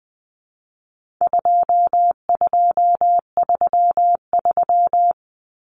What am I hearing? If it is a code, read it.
Morse code, heard as a keyed tone, about 20 wpm, 700 Hz. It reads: 2233
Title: Common Cut Numbers - 4-Digit 20wpm